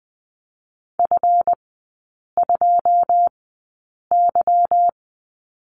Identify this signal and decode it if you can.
Morse code, heard as a keyed tone, about 20 wpm, spaced as for 8 wpm, 690 Hz. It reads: F2Y
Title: 145 Lesson 24e - Letter y - Three Characters 20wpm